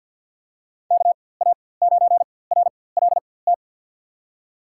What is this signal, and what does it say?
Morse code, heard as a keyed tone, about 52 wpm, spaced as for 23 wpm, 700 Hz. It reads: KA9RLT
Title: USA Call Signs - 6 Characters - 4x Character Spacing - 50wpm